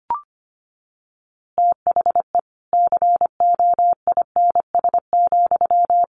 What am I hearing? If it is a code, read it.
Morse code, heard as a keyed tone, about 25 wpm, 700 Hz. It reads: THE COINS,
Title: A Sound Of Thunder 25wpm